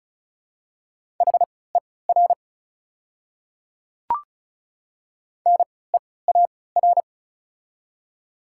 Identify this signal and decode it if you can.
Morse code, heard as a keyed tone, about 34 wpm, spaced as for 20 wpm, 700 Hz. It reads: HER NEAR
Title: 3x Character Spacing - Top 200 Words - 35wpm